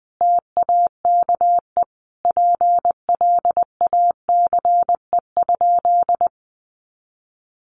Morse code, heard as a keyed tone, about 20 wpm, 700 Hz. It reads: TAKE PLACE?
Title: Sentences from Top 200 Words 20wpm